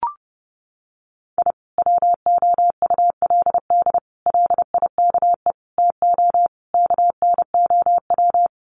Morse code, heard as a keyed tone, about 30 wpm, 700 Hz. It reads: I WOULD LIKE TO KNOW
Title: Sets of 5 Words Encore 30wpm